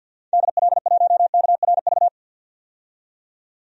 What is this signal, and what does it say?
Morse code, heard as a keyed tone, about 52 wpm, 700 Hz. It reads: DL1KRV
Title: DXCC-100 Call Signs - 6 Characters - 50wpm